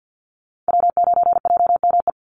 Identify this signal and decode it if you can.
Morse code, heard as a keyed tone, about 50 wpm, 700 Hz. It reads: W9JME
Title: USA Call Signs 50wpm